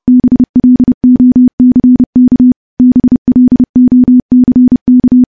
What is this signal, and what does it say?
Morse code, heard as a keyed tone, about 30 wpm, 250 Hz. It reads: BLOCK BLOCK
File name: morse / morse-training-30WPM